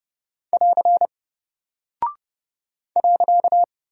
Morse code, heard as a keyed tone, about 30 wpm, 700 Hz. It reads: <AR> .
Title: Single Character (Letter:Number:Punctuation:Prosign) 30wpm